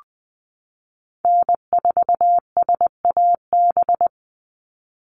Morse code, heard as a keyed tone, about 20 wpm, 700 Hz. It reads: N4SAB
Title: USA Call Signs 20wpm